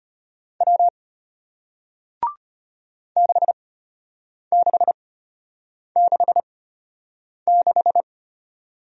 Morse code, heard as a keyed tone, about 33 wpm, spaced as for 8 wpm, 700 Hz. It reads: W 6666